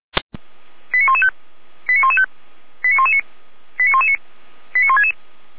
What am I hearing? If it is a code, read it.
Five-tone selective call (ZVEI), heard as quick runs of five tones, tones of about 70 ms. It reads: 89115 89115 89119 89119 89166